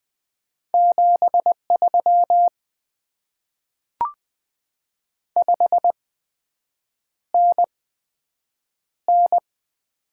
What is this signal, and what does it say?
Morse code, heard as a keyed tone, about 20 wpm, spaced as for 7 wpm, 700 Hz. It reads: 73 5NN
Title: Top 20 QSO Elements - 8x Character Spacing - 20wpm